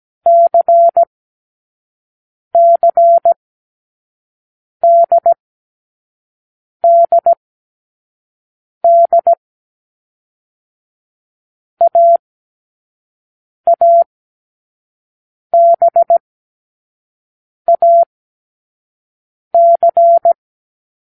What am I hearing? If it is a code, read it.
Morse code, heard as a keyed tone, about 17 wpm, spaced as for 5 wpm, 680 Hz. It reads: CCDDD AABAC
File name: k7qo / Lesson6(A-D)